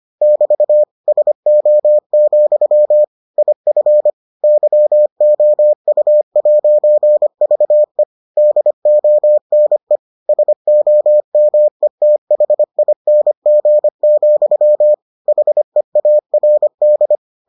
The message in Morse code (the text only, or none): <BT> SO, IF YOU'VE DONE SOMETHING, HEARD